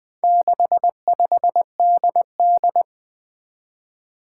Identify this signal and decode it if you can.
Morse code, heard as a keyed tone, about 21 wpm, 700 Hz. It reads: 65DD
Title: Four Challenging Letter-Number Combinations 20wpm